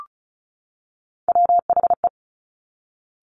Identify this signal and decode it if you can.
Morse code, heard as a keyed tone, about 36 wpm, 700 Hz. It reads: WHE
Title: Top Three Letter Combinations in Words 35wpm